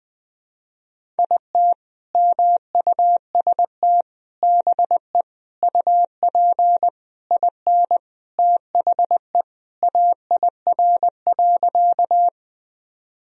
Morse code, heard as a keyed tone, about 21 wpm, 700 Hz. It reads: IT MUST BE UP IN THE AIR.
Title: Sentences from Top 1000 Words - Review Part 1 - 20wpm